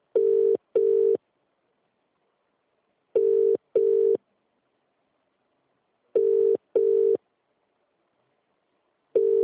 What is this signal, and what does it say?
Ringback tone (UK), 400 + 450 Hz.